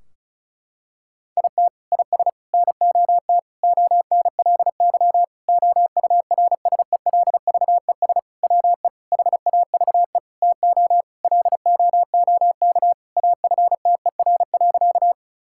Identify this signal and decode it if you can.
Morse code, heard as a keyed tone, about 36 wpm, 700 Hz. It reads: IT IS NOT ONLY OURSELVES WE HAVE TO LOOK AFTER.
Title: Sentences from Top 1600 Words 35wpm